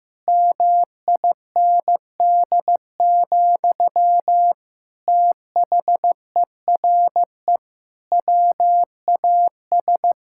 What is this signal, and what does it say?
Morse code, heard as a keyed tone, about 15 wpm, 700 Hz. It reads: MIND, THERE WAS
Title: Sentences from Top 1000 Words - Review Part 5 - 15wpm